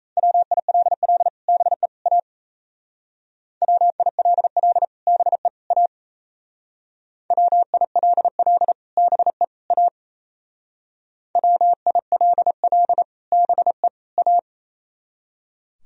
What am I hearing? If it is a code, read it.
Morse code, heard as a keyed tone, about 37 wpm, 700 Hz. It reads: WILL BE A WILL BE A WILL BE A WILL BE A